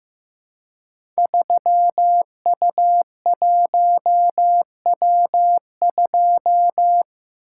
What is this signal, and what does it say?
Morse code, heard as a keyed tone, about 15 wpm, 700 Hz. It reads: 3U1W2
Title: Code Groups - 5 Letter:Numbers - 15wpm